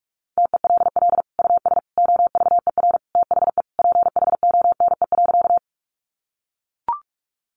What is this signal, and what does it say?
Morse code, heard as a keyed tone, about 44 wpm, 700 Hz. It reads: TELL US OVER THE PHONE.
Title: Sentences from Top 1100 Words 45wpm